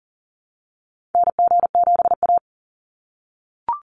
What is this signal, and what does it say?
Morse code, heard as a keyed tone, about 40 wpm, 700 Hz. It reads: NG7A
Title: NA SOTA Activator Callsigns 40wpm